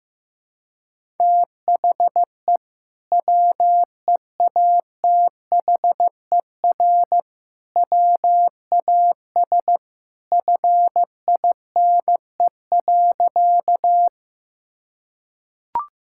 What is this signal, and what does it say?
Morse code, heard as a keyed tone, about 15 wpm, 700 Hz. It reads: THE WEATHER WAS FINE.
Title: Sentences from Top 1000 Words - Review Part 5 - 15wpm